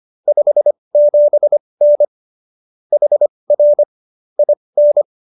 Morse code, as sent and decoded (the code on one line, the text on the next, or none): ..... --... -. / .... .-. / .. -.
57N HR IN